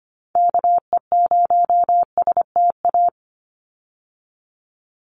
Morse code, heard as a keyed tone, about 25 wpm, 700 Hz. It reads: KE0STA
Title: USA Call Signs 25wpm